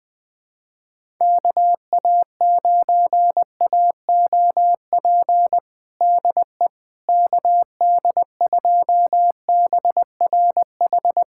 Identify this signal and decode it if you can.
Morse code, heard as a keyed tone, about 20 wpm, 700 Hz. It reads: KA9AOP DE KD2BRH
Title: USA Callsign DE Callsign - 6 Characters - 20wpm